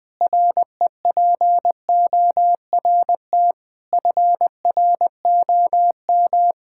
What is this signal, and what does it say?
Morse code, heard as a keyed tone, about 20 wpm, 700 Hz. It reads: REPORT FROM
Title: Sentences with 7-Word Phrases - Spoken Beginning - 20wpm